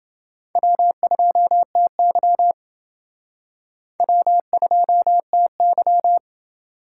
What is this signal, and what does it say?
Morse code, heard as a keyed tone, about 29 wpm, 700 Hz. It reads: W2TY W2TY